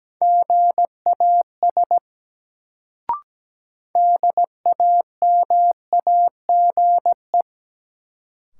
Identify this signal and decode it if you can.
Morse code, heard as a keyed tone, about 17 wpm, 700 Hz. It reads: GAS DAMAGE
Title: Sentences from Top 1100 Words - New Word Drill - 17wpm